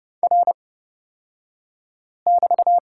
Morse code, heard as a keyed tone, about 30 wpm, spaced as for 5 wpm, 700 Hz. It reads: R<BT>
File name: Lesson 36d - BT - Two Characters 30wpm